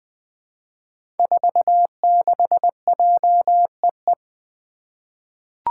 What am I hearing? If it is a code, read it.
Morse code, heard as a keyed tone, about 20 wpm, 700 Hz. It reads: 46JEE
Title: Five Challenging Letter-Number Combinations 20wpm